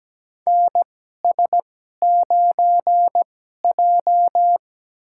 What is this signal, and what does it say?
Morse code, heard as a keyed tone, about 17 wpm, spaced as for 12 wpm, 700 Hz. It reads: NS9J
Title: USA Call Signs - 2x Character Spacing - 17wpm